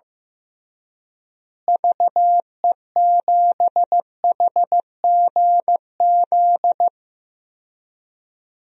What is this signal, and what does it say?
Morse code, heard as a keyed tone, about 15 wpm, 700 Hz. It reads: VE7HGZ
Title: DXCC-100 Call Signs - 6 Characters - Repeated 2x - 15wpm